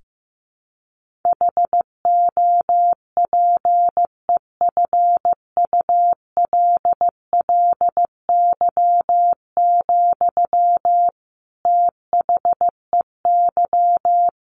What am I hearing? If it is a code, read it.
Morse code, heard as a keyed tone, about 15 wpm, 700 Hz. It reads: HOPEFULLY, THEY
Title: Sentences from Top 1500 Words 15wpm